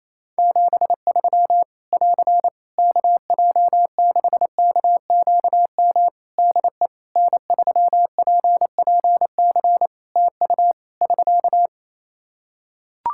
Morse code, heard as a keyed tone, about 28 wpm, 700 Hz. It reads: 73 <AR> KJ6KQM DE N3PPC TU <SK>
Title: K1USN SST QSO Practice 28wpm